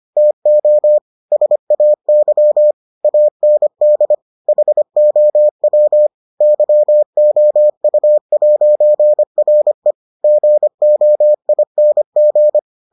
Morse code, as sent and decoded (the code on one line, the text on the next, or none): - --- / ... .- -.-- / .- -. -.. / .... --- .-- / -.-- --- ..- .----. .-. / --. --- .. -. --.
TO SAY AND HOW YOU'RE GOING